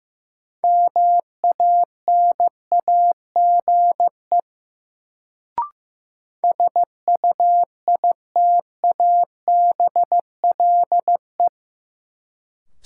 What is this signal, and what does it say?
Morse code, heard as a keyed tone, about 15 wpm, 700 Hz. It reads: MANAGE SUITABLE
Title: Sentences from Top 1400 Words - New Word Drill - 15wpm